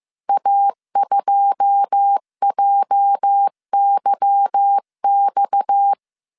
Morse code, heard as a keyed tone, about 15 wpm, 800 Hz. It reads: A2JYX
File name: CW / DET / 1 9 1500C S A 14wpm 3222